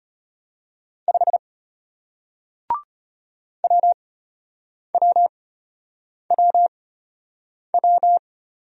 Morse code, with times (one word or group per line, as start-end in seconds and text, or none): 1.1-1.4 5
3.6-8.2 WWWW